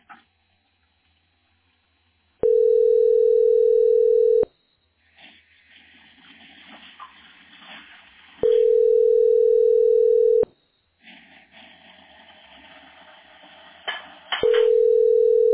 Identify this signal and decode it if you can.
Ringback tone (North America), 440 + 480 Hz.